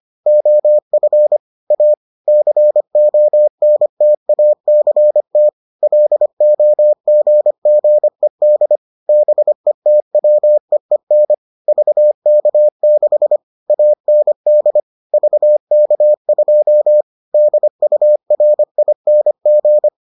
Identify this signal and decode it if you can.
Morse code, heard as a keyed tone, about 25 wpm, 600 Hz. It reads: OF A CONTACT LOGGED BETWEEN VK6 AND VK2 DURING